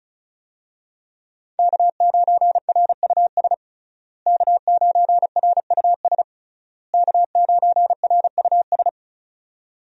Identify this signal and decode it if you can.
Morse code, heard as a keyed tone, about 34 wpm, 700 Hz. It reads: K9RUS K9RUS K9RUS
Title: USA Call Signs - 5 Characters - Repeated 3x - 35wpm